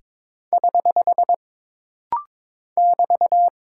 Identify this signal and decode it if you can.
Morse code, heard as a keyed tone, about 22 wpm, 700 Hz. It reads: <HH> <BT>
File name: practice / 22wpm / Single Punctuation-Prosign - Rapid-Fire - 22wpm